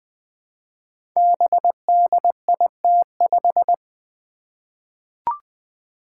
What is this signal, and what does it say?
Morse code, heard as a keyed tone, about 20 wpm, 700 Hz. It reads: BDIT5
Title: Five Challenging Letter-Number Combinations 20wpm